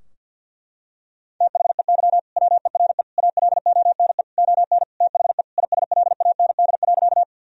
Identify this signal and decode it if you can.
Morse code, heard as a keyed tone, about 50 wpm, 700 Hz. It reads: THEY WERE ALONE ON THE ISLAND.
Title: Sentences from Top 1000 Words - Review Part 3 - 50wpm